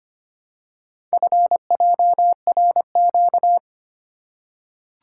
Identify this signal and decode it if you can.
Morse code, heard as a keyed tone, about 25 wpm, 700 Hz. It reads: FJRQ
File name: practice / 25wpm / Four Letter-Number Combinations - 25wpm